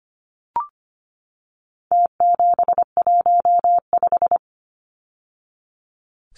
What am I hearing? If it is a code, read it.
Morse code, heard as a keyed tone, about 25 wpm, 700 Hz. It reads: T715
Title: Numbers - Contest Style - 4-Digit 25wpm